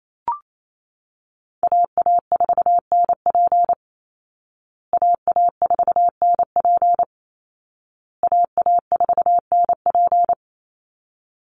Morse code, heard as a keyed tone, about 28 wpm, 700 Hz. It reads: AA4NP AA4NP AA4NP
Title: Call Signs Repeated 2x - 28wpm